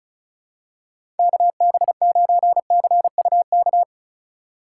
Morse code, heard as a keyed tone, about 34 wpm, 690 Hz. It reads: KB9CUK
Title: USA Call Signs - 6 Characters - 2x Character Spacing - 35wpm